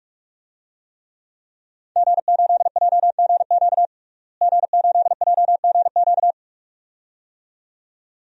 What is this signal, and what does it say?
Morse code, heard as a keyed tone, about 46 wpm, 700 Hz. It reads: G8JGQ G8JGQ
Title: DXCC-100 Call Signs - 5 Characters - Repeated 2x - 45wpm